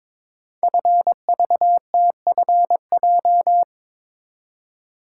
Morse code, heard as a keyed tone, about 23 wpm, 700 Hz. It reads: FVTFJ
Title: Code Groups - 5 Letters - 22wpm